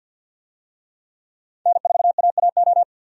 Morse code, heard as a keyed tone, about 47 wpm, 700 Hz. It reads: N4AAO
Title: USA Call Signs - 5 Characters - 50wpm